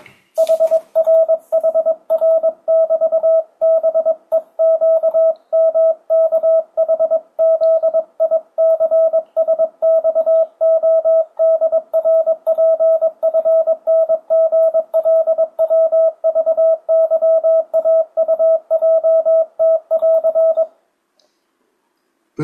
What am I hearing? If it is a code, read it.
Morse code, heard as a keyed tone, about 22 wpm, 650 Hz. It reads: HRHR<BT>BEQMKHZICSXODRPFNGLWVYAUJT<AR>
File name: 2025cw / speedup / 22wpm